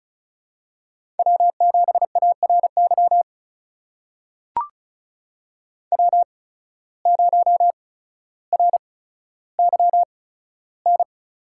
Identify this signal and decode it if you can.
Morse code, heard as a keyed tone, about 35 wpm, spaced as for 15 wpm, 700 Hz. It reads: W7ARY W0RYN
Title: USA Call Signs - 5 Characters - 8x Character Spacing - 35wpm